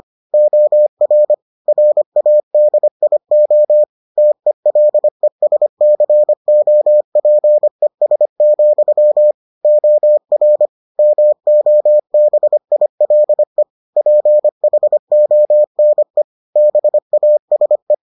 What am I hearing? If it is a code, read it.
Morse code, heard as a keyed tone, about 25 wpm, 600 Hz. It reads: OR RADIO TELESCOPES, OR MOBILE PHONE BASE